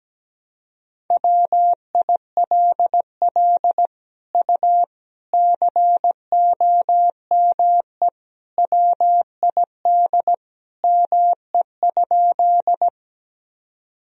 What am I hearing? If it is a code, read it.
Morse code, heard as a keyed tone, about 17 wpm, 700 Hz. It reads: WILL U COME WID ME?
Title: Sentences from Top 200 Words and Abbreviations 17wpm